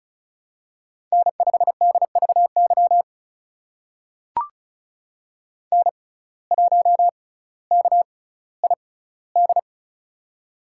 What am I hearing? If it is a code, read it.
Morse code, heard as a keyed tone, about 36 wpm, spaced as for 18 wpm, 700 Hz. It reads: N5DVY N1KID
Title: USA Call Signs - 5 Characters - 6x Character Spacing - 35wpm